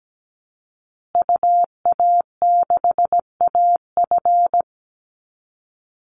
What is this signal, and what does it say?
Morse code, heard as a keyed tone, about 17 wpm, 700 Hz. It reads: UA6AF
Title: Call Signs Repeated 3x - 17wpm